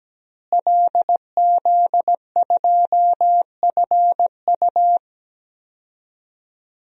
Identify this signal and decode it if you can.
Morse code, heard as a keyed tone, about 17 wpm, 700 Hz. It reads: LZ2FU
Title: DXCC-100 Call Signs - 5 Characters - Repeated 2x - 17wpm